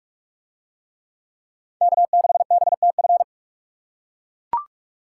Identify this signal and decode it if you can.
Morse code, heard as a keyed tone, about 41 wpm, 700 Hz. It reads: K6BTF